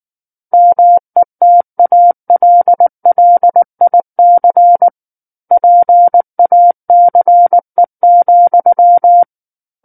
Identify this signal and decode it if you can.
Morse code, heard as a keyed tone, about 19 wpm, 700 Hz. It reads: METALLIC PACE,
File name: Book One / 10-Chapter 10 - 19 WPM